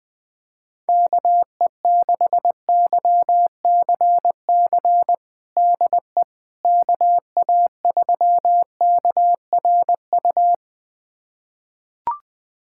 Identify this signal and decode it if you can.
Morse code, heard as a keyed tone, about 20 wpm, 700 Hz. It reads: KE6YCC DE KA3KRU
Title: USA Callsign DE Callsign 20wpm